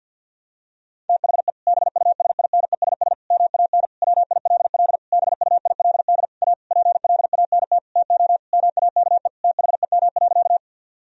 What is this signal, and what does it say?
Morse code, heard as a keyed tone, about 50 wpm, 700 Hz. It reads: THE BUSINESS MAN WILL BUILD A PLANT TO MAKE THEM.
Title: Sentences from Top 1000 Words - Review Part 1 - 50wpm